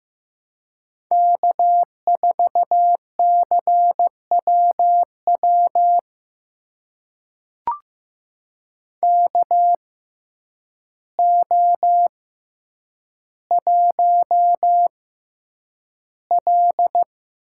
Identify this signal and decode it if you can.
Morse code, heard as a keyed tone, about 15 wpm, spaced as for 8 wpm, 700 Hz. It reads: K4CWW KO1L
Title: USA Call Signs - 6x Character Spacing - 15wpm